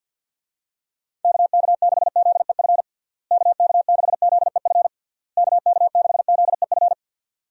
Morse code, heard as a keyed tone, about 52 wpm, 700 Hz. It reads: KK6ZEF KK6ZEF KK6ZEF
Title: USA Call Signs - 6 Characters - Repeated 3x - 50wpm